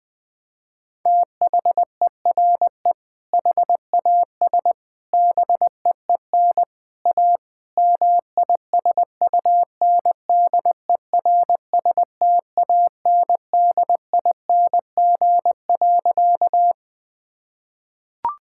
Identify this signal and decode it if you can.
Morse code, heard as a keyed tone, about 20 wpm, 700 Hz. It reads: THERE HAS BEEN A MISUNDERSTANDING.
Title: Sentences with 4-Word Phrases - Spoken Beginning - 20wpm